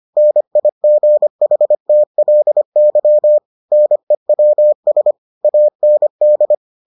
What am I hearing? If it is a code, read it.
Morse code, heard as a keyed tone, about 25 wpm, 600 Hz. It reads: NIGHTLY NEWS AND